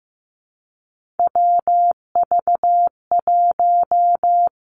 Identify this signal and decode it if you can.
Morse code, heard as a keyed tone, about 15 wpm, 700 Hz. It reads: WV1
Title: Three Challenging Letter-Number Combinations 15wpm